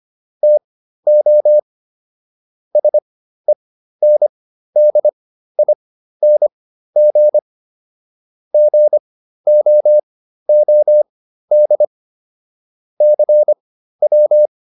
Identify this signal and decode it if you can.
Morse code, heard as a keyed tone, about 25 wpm, spaced as for 13 wpm, 600 Hz. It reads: TO SENDING GOOD CW